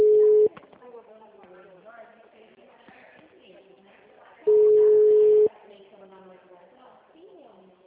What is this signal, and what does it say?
Ringback tone (Europe, 425 Hz), with speech.